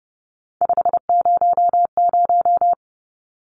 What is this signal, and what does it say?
Morse code, heard as a keyed tone, about 30 wpm, 700 Hz. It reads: 500
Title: Numbers - 3-Digit 30wpm